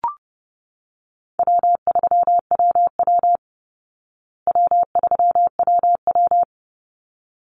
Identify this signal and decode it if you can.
Morse code, heard as a keyed tone, about 30 wpm, 700 Hz. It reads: W3WW W3WW
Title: Easy Call Signs Repeated 2x - 30wpm